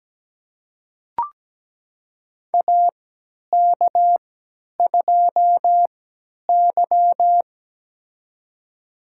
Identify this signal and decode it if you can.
Morse code, heard as a keyed tone, about 17 wpm, spaced as for 10 wpm, 700 Hz. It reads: AK2Y
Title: USA Call Signs - 4 Characters - 3x Character Spacing - 17wpm